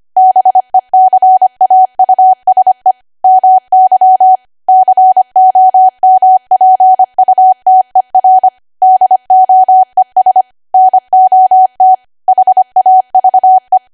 Morse code, heard as a keyed tone, about 25 wpm, 750 Hz. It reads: BECAUSE MY COMPUTER DOES NOT HAVE
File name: Archive / 25 WPM / 25WPM